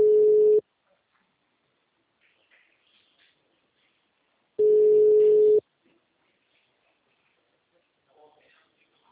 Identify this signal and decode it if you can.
Ringback tone (Europe, 425 Hz), with speech.